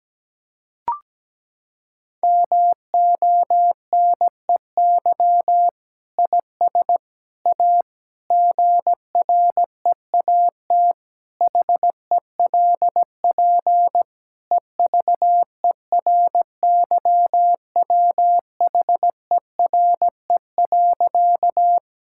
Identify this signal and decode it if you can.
Morse code, heard as a keyed tone, about 17 wpm, 700 Hz. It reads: MONEY IS A GREAT HELP EVERYWHERE.
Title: Sentences from Top 1600 Words 17wpm